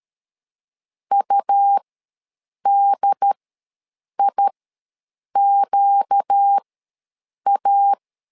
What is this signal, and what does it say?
Morse code, heard as a keyed tone, about 13 wpm, spaced as for 7 wpm, 800 Hz. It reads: UDIQA
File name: LEZIONI / 4 Q ws 12wpm 1912